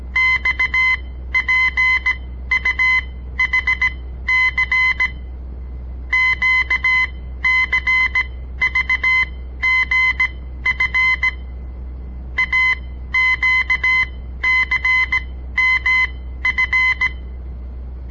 Morse code, heard as a keyed tone, about 17 wpm, spaced as for 13 wpm, 2050 Hz. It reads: XPUHC QCVGF AQCMF
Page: On the air Morse code practice from the Southern Montana Amatuer Radio Association; all code, no read back